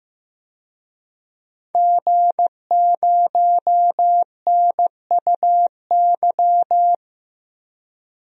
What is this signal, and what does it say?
Morse code, heard as a keyed tone, about 15 wpm, 700 Hz. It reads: G0NUY
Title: DXCC-100 Call Signs - 5 Characters - Repeated 2x - 15wpm